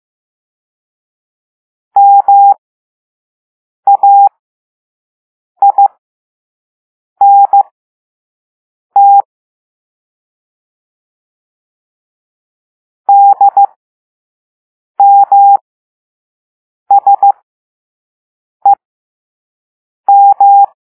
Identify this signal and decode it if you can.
Morse code, heard as a keyed tone, about 15 wpm, spaced as for 5 wpm, 800 Hz. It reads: MAINT DMSEM